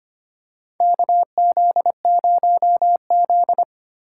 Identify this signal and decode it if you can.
Morse code, heard as a keyed tone, about 25 wpm, 700 Hz. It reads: KZ0Z
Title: USA Call Signs - 4 Characters - 4x Character Spacing - 25wpm